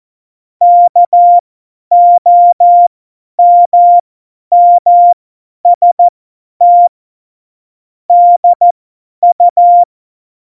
Morse code, heard as a keyed tone, about 14 wpm, spaced as for 10 wpm, 700 Hz. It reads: KOMMST DU
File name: Morsekurs / Witze / Witz 21 April 2020, 50 F70